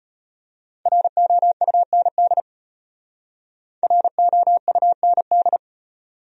Morse code, heard as a keyed tone, about 32 wpm, 700 Hz. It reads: ROUND ROUND